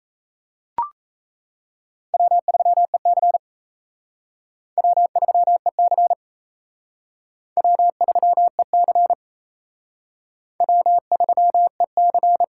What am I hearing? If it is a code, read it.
Morse code, heard as a keyed tone, about 34 wpm, 700 Hz. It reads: W3EC W3EC W3EC W3EC
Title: Speed Racing - USA 1x2 Call Signs 28wpm